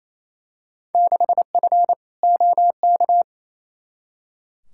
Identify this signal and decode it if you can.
Morse code, heard as a keyed tone, about 28 wpm, 700 Hz. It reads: 6F OK
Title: Field Day Exchanges Encore - 28wpm